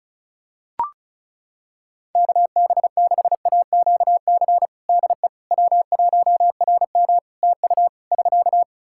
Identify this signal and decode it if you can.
Morse code, heard as a keyed tone, about 36 wpm, 700 Hz. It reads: KB6AQC DE W1RM TU <SK>
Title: Practice QSO - Exchange 4 - Ending - 35wpm